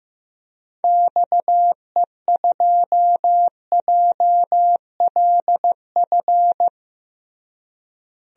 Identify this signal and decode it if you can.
Morse code, heard as a keyed tone, about 15 wpm, 700 Hz. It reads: XE2JLF
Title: DXCC-100 Call Signs - 6 Characters - Repeated 3x - 15wpm